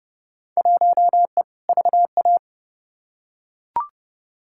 Morse code, heard as a keyed tone, about 30 wpm, 700 Hz. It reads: 1E VA
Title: Field Day Exchanges - 30wpm